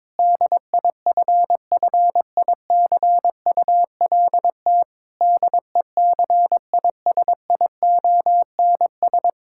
Morse code, heard as a keyed tone, about 22 wpm, 700 Hz. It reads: DIFFICULT DECISIONS
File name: Difficult Sentences with Hints - 22wpm